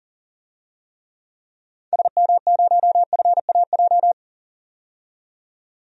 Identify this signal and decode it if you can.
Morse code, heard as a keyed tone, about 38 wpm, 700 Hz. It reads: SM0FAJ
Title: DXCC-100 Call Signs - 6 Characters - 40wpm